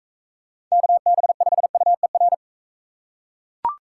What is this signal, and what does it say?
Morse code, heard as a keyed tone, about 43 wpm, 700 Hz. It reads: KB5UER